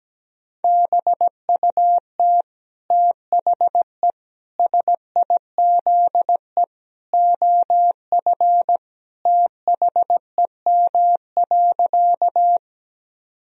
Morse code, heard as a keyed tone, about 17 wpm, 700 Hz. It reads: BUT THE SIZE OF THEM.
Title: Sentences from Top 1000 Words - Review Part 2 - 17wpm